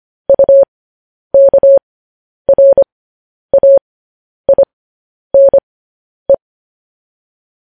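Morse code, heard as a keyed tone, about 25 wpm, spaced as for 10 wpm, 550 Hz. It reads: UKRAINE
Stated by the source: News headlines sent in Morse code at 10 wpm.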